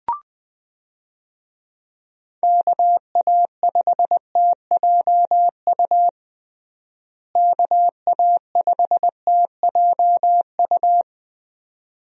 Morse code, heard as a keyed tone, about 20 wpm, 700 Hz. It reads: KA5TJU KA5TJU
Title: USA Call Signs - 6 Characters - Repeated 2x - 20wpm